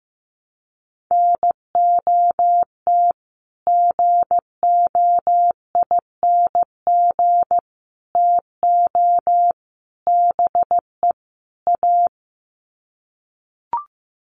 Morse code, heard as a keyed tone, about 15 wpm, 700 Hz. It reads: NOT GOING TO BE A
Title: Sets of 5 Words Encore 15wpm